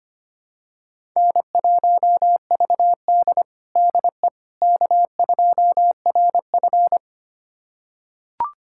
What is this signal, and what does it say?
Morse code, heard as a keyed tone, about 25 wpm, 700 Hz. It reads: N1VD DE K2RF
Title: USA Callsign DE Callsign - 4 Characters - 25wpm